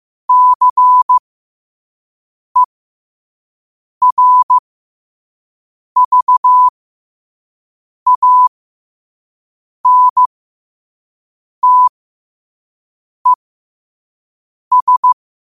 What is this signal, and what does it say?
Morse code, heard as a keyed tone, about 15 wpm, spaced as for 5 wpm, 1000 Hz. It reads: CERVANTES
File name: qotdfiles / 5wpm